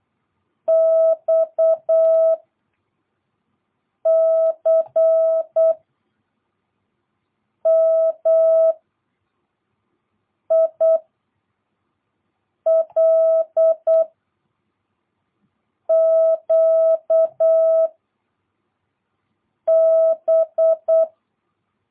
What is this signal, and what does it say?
Morse code, heard as a keyed tone, about 8 wpm, 660 Hz. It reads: X C M I L Q B